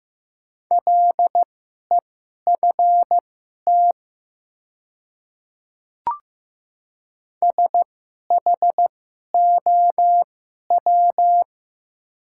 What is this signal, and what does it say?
Morse code, heard as a keyed tone, about 15 wpm, spaced as for 11 wpm, 700 Hz. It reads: LEFT SHOW
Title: 2x Character Spacing - Top 300 Words - 15wpm